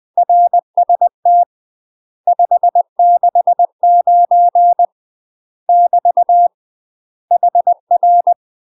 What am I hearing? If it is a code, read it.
Morse code, heard as a keyed tone, about 20 wpm, 700 Hz. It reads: RST 569 <BT> HR
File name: UK MASTER QSOs TRAINING / 20WPM / UK MASTER QSOs 0013